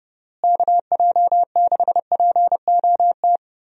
Morse code, heard as a keyed tone, about 32 wpm, 700 Hz. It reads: KJ6POT
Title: USA Call Signs - 6 Characters - Repeated 2x - 30wpm